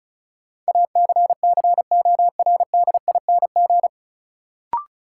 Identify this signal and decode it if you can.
Morse code, heard as a keyed tone, about 36 wpm, 700 Hz. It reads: ACCORDING
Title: Sentences from Top 700 Words - New Word Drill - 35wpm